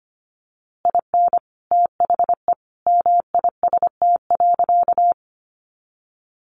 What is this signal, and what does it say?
Morse code, heard as a keyed tone, about 25 wpm, 700 Hz. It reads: IN THE MIST.
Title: A Sound Of Thunder 25wpm